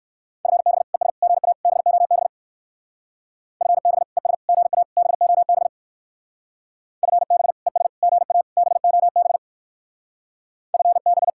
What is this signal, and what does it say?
Morse code, heard as a keyed tone, about 56 wpm, 700 Hz. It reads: FB ES GA BOB FB ES GA BOB FB ES GA BOB FB
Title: Speed-Racing Common QSO Phrases - 45wpm